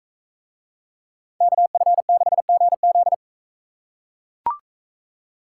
Morse code, heard as a keyed tone, about 42 wpm, 700 Hz. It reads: KF6GZ